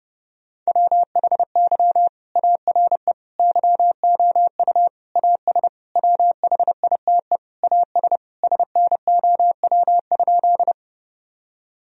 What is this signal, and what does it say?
Morse code, heard as a keyed tone, about 32 wpm, 700 Hz. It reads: WHY ARE YOU AS WHITE AS SNOW?
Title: Sentences from Top 1000 Words - Review Part 3 - 30wpm